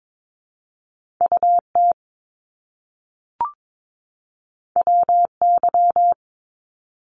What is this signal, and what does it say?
Morse code, heard as a keyed tone, about 22 wpm, 700 Hz. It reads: UT WY
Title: ARRL Sections 22wpm